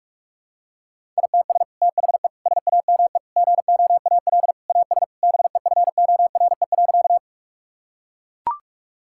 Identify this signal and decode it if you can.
Morse code, heard as a keyed tone, about 41 wpm, 700 Hz. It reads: ITS THE SAME GOAL AS BEFORE.
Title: Sentences from Top 1000 Words - Review Part 5 - 45wpm